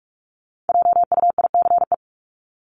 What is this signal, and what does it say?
Morse code, heard as a keyed tone, about 44 wpm, 700 Hz. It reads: JUICE
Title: Top 1251-1500 Words 45wpm